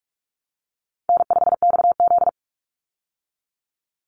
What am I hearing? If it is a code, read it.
Morse code, heard as a keyed tone, about 44 wpm, 700 Hz. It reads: N5XZ
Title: Call Signs Repeated 3x - 45wpm